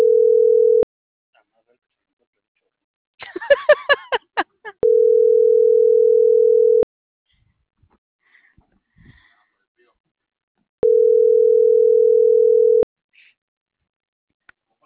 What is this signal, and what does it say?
Ringback tone (North America), 440 + 480 Hz.